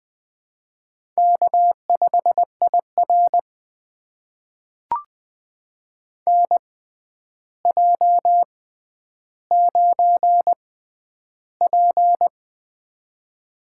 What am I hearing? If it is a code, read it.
Morse code, heard as a keyed tone, about 20 wpm, spaced as for 10 wpm, 700 Hz. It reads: K5IR NJ9P
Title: USA Call Signs - 6x Character Spacing - 20wpm